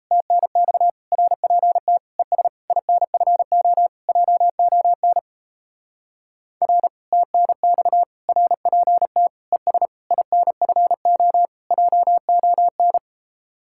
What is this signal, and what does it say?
Morse code, heard as a keyed tone, about 36 wpm, 700 Hz. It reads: TNX RPT ES INFO JON R TNX RPT ES INFO JON